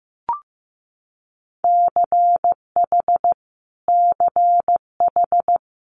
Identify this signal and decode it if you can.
Morse code, heard as a keyed tone, about 15 wpm, 700 Hz. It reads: CH CH
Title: Multi-letter Phonemes Repeated 3x 15wpm